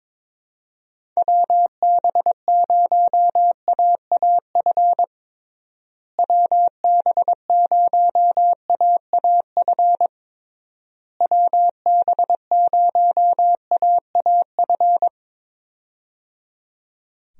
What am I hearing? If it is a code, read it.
Morse code, heard as a keyed tone, about 21 wpm, 700 Hz. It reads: WB0AAF WB0AAF WB0AAF
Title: USA Call Signs - 6 Characters - Repeated 3x - 22wpm